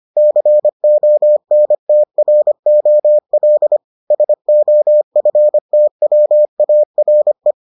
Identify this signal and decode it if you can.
Morse code, heard as a keyed tone, about 25 wpm, 600 Hz. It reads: CONTROL SOFTWARE